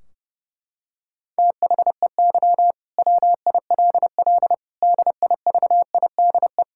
Morse code, heard as a keyed tone, about 32 wpm, 700 Hz. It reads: THEY WILL DIVIDE
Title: Sentences from Top 1600 Words 30wpm